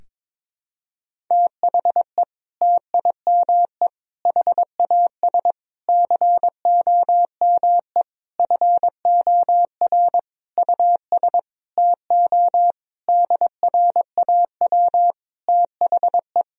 Morse code, heard as a keyed tone, about 22 wpm, 700 Hz. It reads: THE TIME HAS COME FOR US TO DRAW THE
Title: Sentences from Top 1000 Words - Review Part 1 - 22wpm